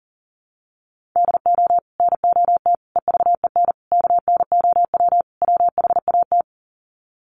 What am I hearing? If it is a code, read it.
Morse code, heard as a keyed tone, about 40 wpm, 700 Hz. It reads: DO NOT EVEN KNOW WHAT
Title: Sets of 5 Words Encore 40wpm